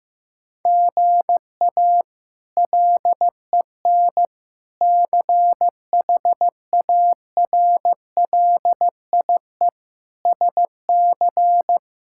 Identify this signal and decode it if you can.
Morse code, heard as a keyed tone, about 16 wpm, 700 Hz. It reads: GA LEN CHARLIE SC
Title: K1USN SST QSO Practice - 15wpm